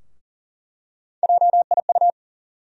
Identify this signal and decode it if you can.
Morse code, heard as a keyed tone, about 40 wpm, 700 Hz. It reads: JIU
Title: USA Call Sign Postfixes - 3 Letters 40wpm